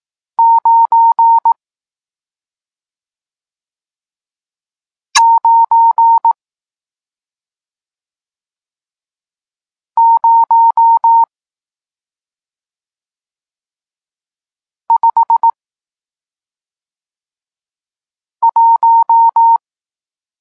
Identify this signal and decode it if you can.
Morse code, heard as a keyed tone, about 18 wpm, spaced as for 2 wpm, 920 Hz. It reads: 99051
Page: Tekst 40 – Nauka znaków przestankowych